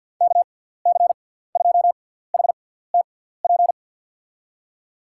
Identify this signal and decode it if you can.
Morse code, heard as a keyed tone, about 53 wpm, spaced as for 17 wpm, 700 Hz. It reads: KC2HTP
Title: USA Call Signs - 6 Characters - 6x Character Spacing - 50wpm